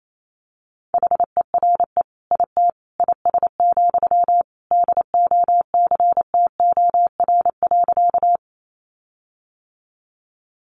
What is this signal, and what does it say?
Morse code, heard as a keyed tone, about 28 wpm, 700 Hz. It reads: HERE IT IS, DOCTOR.
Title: Sentences from Top 1500 Words 28wpm